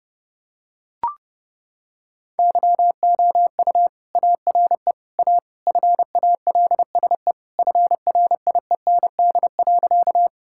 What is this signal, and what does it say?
Morse code, heard as a keyed tone, about 32 wpm, 700 Hz. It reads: YOU ARE A FALSE FRIEND.
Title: Sentences from Top 1600 Words 30wpm